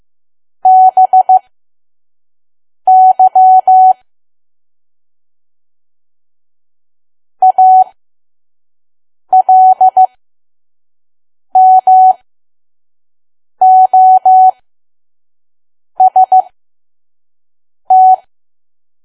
Morse code, heard as a keyed tone, about 15 wpm, spaced as for 5 wpm, 750 Hz. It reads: BY ALMOST
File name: Archive / 5 WPM files / PM